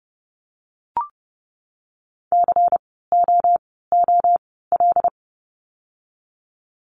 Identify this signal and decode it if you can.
Morse code, heard as a keyed tone, about 30 wpm, 700 Hz. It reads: C O O L
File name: Lesson 16f - Letter c - Words 30wpm